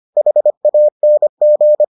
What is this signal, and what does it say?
Morse code, heard as a keyed tone, about 25 wpm, 600 Hz. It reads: HANG